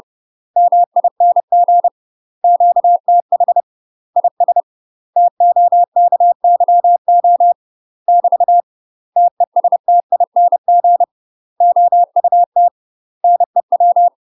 Morse code, heard as a keyed tone, about 30 wpm, 700 Hz. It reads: MING QTH IS TOKYO <BT> TESTING OUT NEW
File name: UK MASTER QSOs TRAINING / 30WPM / UK MASTER QSOs 0030